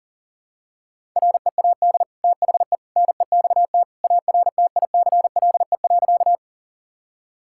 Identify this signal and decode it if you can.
Morse code, heard as a keyed tone, about 37 wpm, 700 Hz. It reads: READ THE NEXT ARTICLE.
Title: Sentences from Top 1000 Words - Review Part 5 - 40wpm